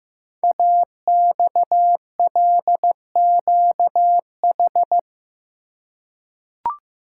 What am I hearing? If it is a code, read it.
Morse code, heard as a keyed tone, about 15 wpm, 700 Hz. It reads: AXLQH
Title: Code Groups - 5 Letters - 15wpm